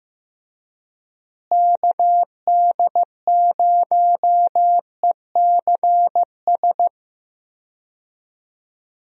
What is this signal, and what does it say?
Morse code, heard as a keyed tone, about 15 wpm, 700 Hz. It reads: KD0ECS
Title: USA Call Signs - 6 Characters - Repeated 3x - 15wpm